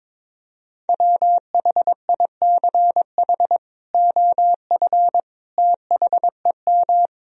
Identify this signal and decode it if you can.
Morse code, heard as a keyed tone, about 22 wpm, 700 Hz. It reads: WHICH OF THEM
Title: Sentences from Top 1000 Words - Review Part 3 - 22wpm